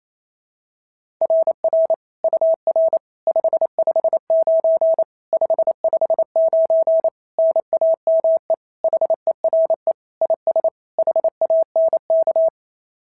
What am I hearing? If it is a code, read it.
Morse code, heard as a keyed tone, about 25 wpm, 650 Hz. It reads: RR UR 559 559 NAME HERE IS HANK
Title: SOTA QSO Practice 28wpm